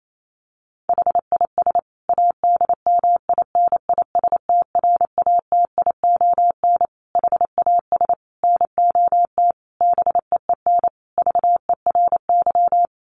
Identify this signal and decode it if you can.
Morse code, heard as a keyed tone, about 28 wpm, 700 Hz. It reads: HIS ADMINISTRATION HAS NOT BEEN VERY
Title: Sentences from Top 1300 Words 28wpm